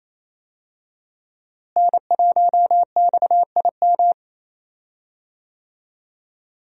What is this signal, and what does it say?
Morse code, heard as a keyed tone, about 28 wpm, 700 Hz. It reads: N1XIM
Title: USA Call Signs - 5 Characters - Repeated 2x - 28wpm